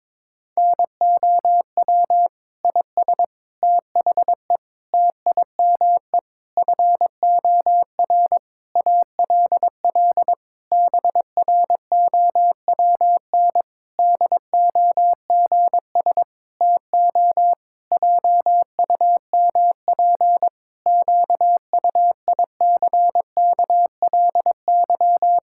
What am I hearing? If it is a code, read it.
Morse code, heard as a keyed tone, about 22 wpm, 700 Hz. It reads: NOW IS THE TIME FOR ALL BROWN DOGS TO JUMP QUICKLY
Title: Pangram Challenge v2 - 22wpm